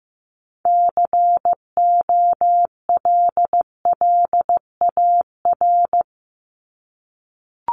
Morse code, heard as a keyed tone, about 15 wpm, 700 Hz. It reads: COLLAR
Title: Top 1751-2000 Words 15wpm